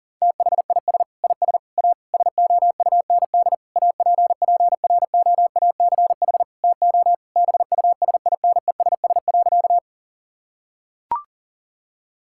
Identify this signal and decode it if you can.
Morse code, heard as a keyed tone, about 40 wpm, 700 Hz. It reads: THIS IS A SOUND APPROACH TO BUSINESS.
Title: Sentences from Top 1000 Words - Review Part 6 - 40wpm